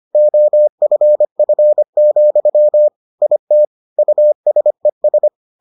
Morse code, heard as a keyed tone, about 25 wpm, 600 Hz. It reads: OFF, IT USES